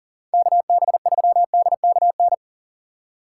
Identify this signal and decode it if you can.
Morse code, heard as a keyed tone, about 49 wpm, 700 Hz. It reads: KB3DKN
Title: Speed Racing - USA 2x3 Call Signs 40wpm